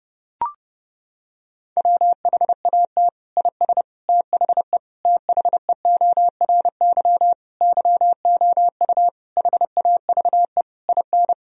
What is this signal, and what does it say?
Morse code, heard as a keyed tone, about 30 wpm, 700 Hz. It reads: WHAT IS THE THEORY YOU HAVE IN
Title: Sentences from Top 1000 Words - Review Part 3 - 30wpm